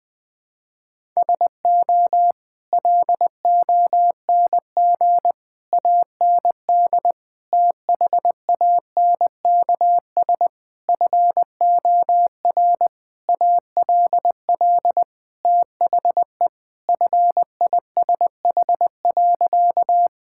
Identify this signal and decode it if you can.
Morse code, heard as a keyed tone, about 19 wpm, 700 Hz. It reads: SO LONG AND THANKS FOR ALL THE FISH.
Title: Sentences with 8-Word Phrases - Spoken Beginning - 20wpm